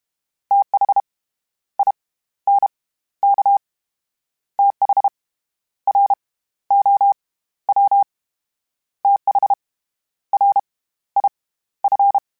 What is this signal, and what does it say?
Morse code, heard as a keyed tone, about 32 wpm, 800 Hz. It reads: TH I N K TH R O W TH R I F